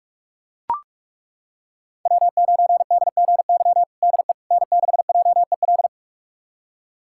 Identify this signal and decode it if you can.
Morse code, heard as a keyed tone, about 42 wpm, 700 Hz. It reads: W9DGY DE N6JEL
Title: USA Callsign DE Callsign - 5 Characters - 45wpm